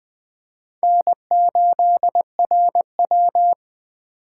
Morse code, heard as a keyed tone, about 20 wpm, 700 Hz. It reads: N8RW
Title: Speed Racing - USA Call Signs 17wpm